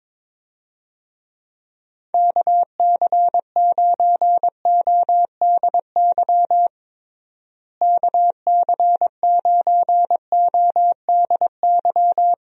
Morse code, heard as a keyed tone, about 22 wpm, 700 Hz. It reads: KC9ODY KC9ODY
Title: USA Call Signs - 6 Characters - Repeated 3x - 22wpm